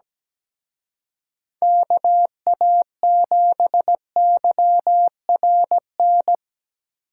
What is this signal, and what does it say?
Morse code, heard as a keyed tone, about 20 wpm, 700 Hz. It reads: KA7YRN
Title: Speed Racing - USA 2x3 Call Signs 17wpm